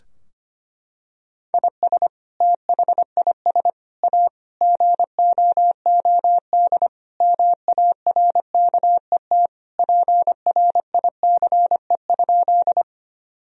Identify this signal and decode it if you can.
Morse code, heard as a keyed tone, about 25 wpm, 700 Hz. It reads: IS THIS A GOOD MARKET PRICE?
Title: Sentences from Top 1000 Words - Review Part 3 - 25wpm